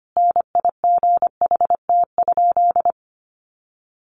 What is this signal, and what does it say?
Morse code, heard as a keyed tone, about 25 wpm, 700 Hz. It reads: NIGHT?
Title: Sentences from Top 600 Words 25wpm